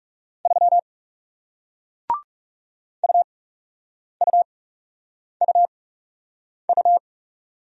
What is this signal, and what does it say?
Morse code, heard as a keyed tone, about 41 wpm, spaced as for 8 wpm, 700 Hz. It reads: 3 UUUU